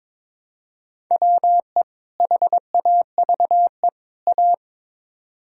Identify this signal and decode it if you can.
Morse code, heard as a keyed tone, about 22 wpm, 700 Hz. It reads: WE HAVE A
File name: practice / 22wpm / Build A Phrase - 5-Word Phrases - 22wpm